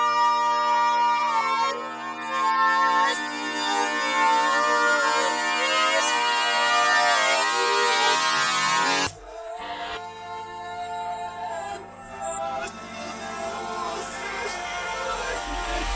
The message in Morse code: none